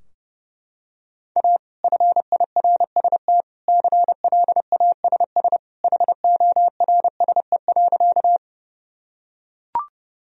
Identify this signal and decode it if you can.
Morse code, heard as a keyed tone, about 32 wpm, 700 Hz. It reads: A FIRST CLASS HORSE.
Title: Sentences from Top 1000 Words - Review Part 2 - 30wpm